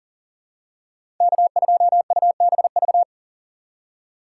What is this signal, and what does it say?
Morse code, heard as a keyed tone, about 39 wpm, 700 Hz. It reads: K2UBV
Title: USA Call Signs - 5 Characters - 4x Character Spacing - 40wpm